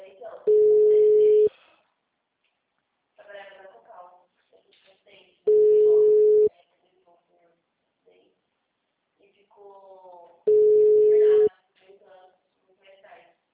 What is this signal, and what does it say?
Ringback tone (Europe, 425 Hz), with speech.
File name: out- -2006- - - .0